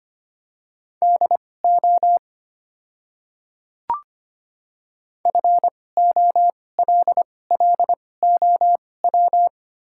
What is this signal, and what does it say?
Morse code, heard as a keyed tone, about 25 wpm, spaced as for 18 wpm, 700 Hz. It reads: DO FOLLOW
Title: 2x Character Spacing - Top 300 Words - 25wpm